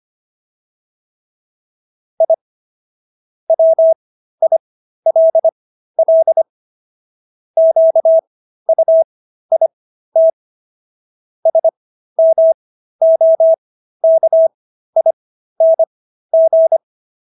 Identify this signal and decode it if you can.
Morse code, heard as a keyed tone, about 25 wpm, spaced as for 13 wpm, 650 Hz. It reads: I WILL QUIT SMOKING
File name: SS106 13